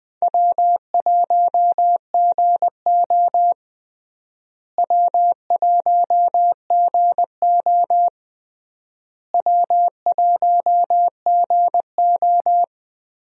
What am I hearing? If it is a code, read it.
Morse code, heard as a keyed tone, about 20 wpm, 700 Hz. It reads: W1GO W1GO W1GO
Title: USA Call Signs - 4 Characters - Repeated 3x - 20wpm